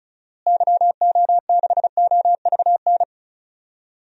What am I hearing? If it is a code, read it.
Morse code, heard as a keyed tone, about 36 wpm, 700 Hz. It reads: YO6OVN
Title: DXCC-100 Call Signs - 6 Characters - Repeated 2x - 35wpm